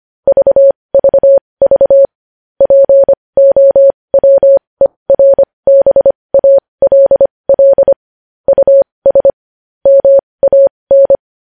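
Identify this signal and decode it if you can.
Morse code, heard as a keyed tone, about 25 wpm, spaced as for 20 wpm, 550 Hz. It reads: VVV POWERBALL US MAN
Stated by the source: Morse code news headlines practice at 20 wpm.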